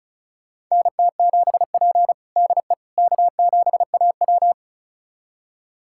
Morse code, heard as a keyed tone, about 36 wpm, 700 Hz. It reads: NT7P DE K7AW
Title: USA Callsign DE Callsign - 4 Characters - 35wpm